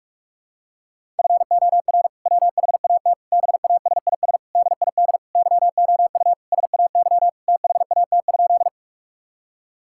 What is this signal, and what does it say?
Morse code, heard as a keyed tone, about 42 wpm, 700 Hz. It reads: FOR WHAT BASIS DID YOU SAY THAT?
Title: Sentences from Top 1000 Words - Review Part 5 - 45wpm